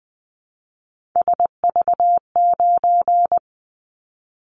Morse code, heard as a keyed tone, about 20 wpm, 700 Hz. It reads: SV9
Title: Callsign Prefixes for Top 100 DXCC - 20wpm